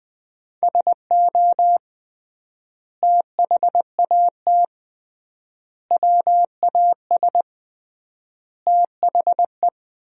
Morse code, heard as a keyed tone, about 20 wpm, 700 Hz. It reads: SO THAT WAS THE
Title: 3x Word-Spacing - Sentences from Top 100 Words - 20wpm